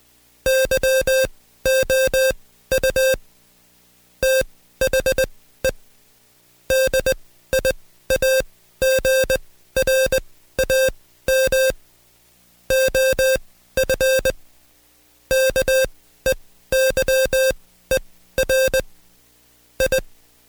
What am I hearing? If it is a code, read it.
Morse code, heard as a keyed tone, about 20 wpm, spaced as for 14 wpm, 530 Hz. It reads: YOU THE DIAGRAM OF KEYER I
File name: Code Quick / AudioDownld2 / Seg133-Stepper6 13WPM